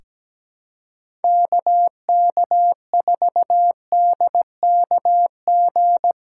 Morse code, heard as a keyed tone, about 17 wpm, 700 Hz. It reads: KK4DKG
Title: USA Callsign DE Callsign 17wpm